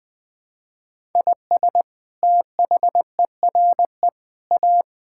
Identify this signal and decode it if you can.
Morse code, heard as a keyed tone, about 21 wpm, 700 Hz. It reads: IS THERE A
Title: Sentences from Top 1000 Words - Review Part 6 - 20wpm